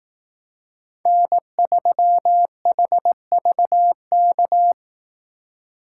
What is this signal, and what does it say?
Morse code, heard as a keyed tone, about 18 wpm, 700 Hz. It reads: N3HVK
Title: Speed Racing - USA Call Signs 15wpm